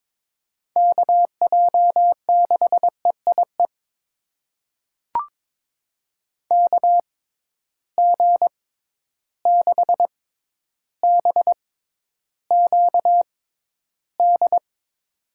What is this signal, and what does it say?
Morse code, heard as a keyed tone, about 22 wpm, spaced as for 11 wpm, 700 Hz. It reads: KJ6EIE KG6BQD
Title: USA Call Signs - 6x Character Spacing - 22wpm